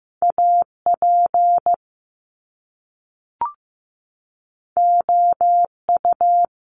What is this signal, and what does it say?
Morse code, heard as a keyed tone, about 15 wpm, 700 Hz. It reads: AP OU
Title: Top Two Letter Combinations in Words 15wpm